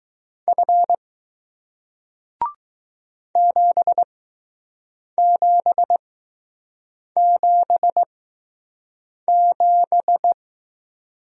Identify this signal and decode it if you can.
Morse code, heard as a keyed tone, about 20 wpm, spaced as for 6 wpm, 700 Hz. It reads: F 7777